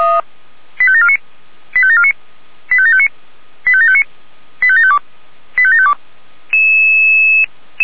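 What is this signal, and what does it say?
Five-tone selective call (ZVEI), heard as quick runs of five tones, tones of about 70 ms. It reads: 85749 85749 85759 85759 85752 85752
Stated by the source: ZVEI five-tone selective call received by SDR.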